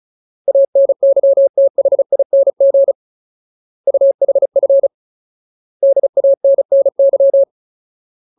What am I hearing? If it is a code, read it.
Morse code, heard as a keyed tone, about 34 wpm, 550 Hz. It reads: ANYTHING UHF DANNY
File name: Ultimate Word List / 35WPM Ultimate